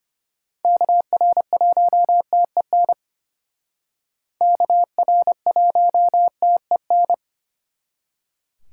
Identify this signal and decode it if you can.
Morse code, heard as a keyed tone, about 32 wpm, 700 Hz. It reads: KR1TEN KR1TEN